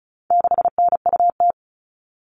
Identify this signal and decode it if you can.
Morse code, heard as a keyed tone, about 34 wpm, 700 Hz. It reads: 6NUT
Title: All Cut Numbers - 4-Digit 35wpm